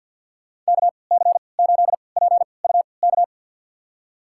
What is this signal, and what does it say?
Morse code, heard as a keyed tone, about 52 wpm, spaced as for 29 wpm, 690 Hz. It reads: KC8PUK
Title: USA Call Signs - 6 Characters - 3x Character Spacing - 50wpm